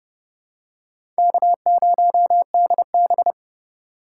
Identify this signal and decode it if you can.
Morse code, heard as a keyed tone, about 30 wpm, 700 Hz. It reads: K0DB
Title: USA Call Signs - 4 Characters - 4x Character Spacing - 30wpm